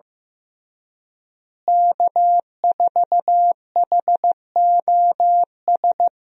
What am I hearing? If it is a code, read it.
Morse code, heard as a keyed tone, about 15 wpm, 700 Hz. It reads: K4HOS
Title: USA Call Signs - 5 Characters - Repeated 3x - 15wpm